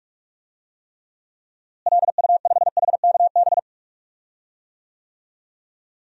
Morse code, heard as a keyed tone, about 41 wpm, 700 Hz. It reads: LU5HKB
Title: DXCC-100 Call Signs - 6 Characters - Repeated 3x - 45wpm